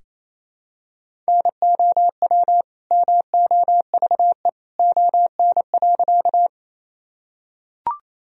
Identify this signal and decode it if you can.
Morse code, heard as a keyed tone, about 28 wpm, 700 Hz. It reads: NOW MOVE ON.
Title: Sentences from Top 1000 Words - Review Part 1 - 28wpm